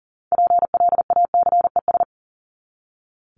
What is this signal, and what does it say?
Morse code, heard as a keyed tone, about 40 wpm, 700 Hz. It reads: PLACES
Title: ES:S Suffix - Top 100 Words with Base Words 40wpm